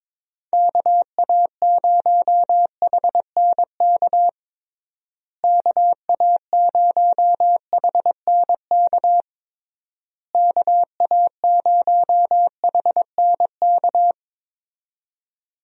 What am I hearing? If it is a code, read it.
Morse code, heard as a keyed tone, about 22 wpm, 700 Hz. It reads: KA0HNK KA0HNK KA0HNK
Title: USA Call Signs - 6 Characters - Repeated 3x - 22wpm